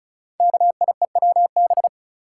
Morse code, heard as a keyed tone, about 36 wpm, 690 Hz. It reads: KIEWB
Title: Code Groups - 5 Letters - 35wpm